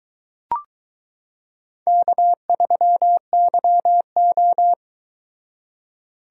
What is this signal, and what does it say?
Morse code, heard as a keyed tone, about 22 wpm, 700 Hz. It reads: K3YO